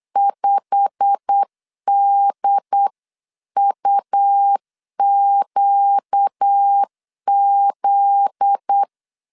Morse code, heard as a keyed tone, about 8 wpm, 800 Hz. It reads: 5DUQZ
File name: CW / DET / 1 9 1500C S C 8wpm 5653